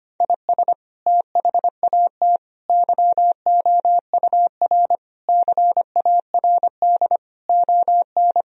Morse code, heard as a keyed tone, about 25 wpm, 700 Hz. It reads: IS THAT YOUR CARD ON
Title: Sentences from Top 1000 Words - Review Part 4 - 25wpm